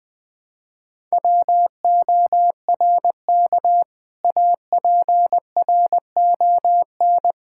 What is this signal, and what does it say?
Morse code, heard as a keyed tone, about 20 wpm, 700 Hz. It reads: WORK APRON
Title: Difficult Words - Speed-Racing with Hints - 20wpm